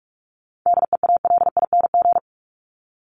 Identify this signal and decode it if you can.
Morse code, heard as a keyed tone, about 44 wpm, 700 Hz. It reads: DEALING
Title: ING Suffix - Top 100 Words with Base Words 45wpm